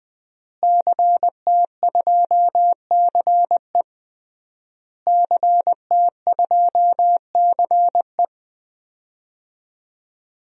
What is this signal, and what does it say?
Morse code, heard as a keyed tone, about 21 wpm, 700 Hz. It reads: CT2CE CT2CE
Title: DXCC-100 Call Signs - 5 Characters - Repeated 3x - 20wpm